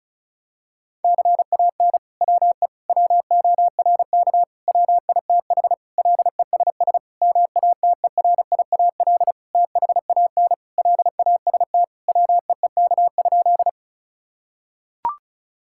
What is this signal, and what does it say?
Morse code, heard as a keyed tone, about 34 wpm, 700 Hz. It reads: CAN WE WORK WITH LESS MATERIAL THAN LAST WEEK?
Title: Sentences from Top 1000 Words - Review Part 5 - 35wpm